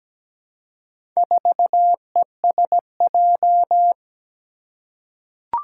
Morse code, heard as a keyed tone, about 17 wpm, 700 Hz. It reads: 4ESJ
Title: Four Challenging Letter-Number Combinations 17wpm